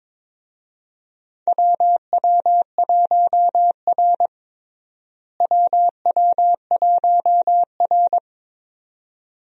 Morse code, heard as a keyed tone, about 22 wpm, 700 Hz. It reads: WW1R WW1R
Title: USA Call Signs - 4 Characters - Repeated 2x - 22wpm